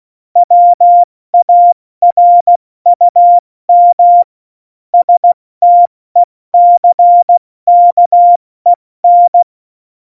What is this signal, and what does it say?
Morse code, heard as a keyed tone, about 16 wpm, spaced as for 14 wpm, 700 Hz. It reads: WARUM STECKEN
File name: Morsekurs / Witze / Witz 01. Mai 2018, 70 F80